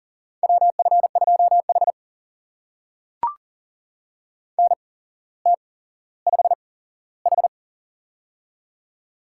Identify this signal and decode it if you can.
Morse code, heard as a keyed tone, about 38 wpm, spaced as for 17 wpm, 700 Hz. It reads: WF2H NT5H
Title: USA Call Signs - 4 Characters - 8x Character Spacing - 40wpm